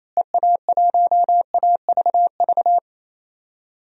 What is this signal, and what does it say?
Morse code, heard as a keyed tone, about 27 wpm, 700 Hz. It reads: EA1AVV
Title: DXCC-100 Call Signs - 6 Characters - Repeated 3x - 28wpm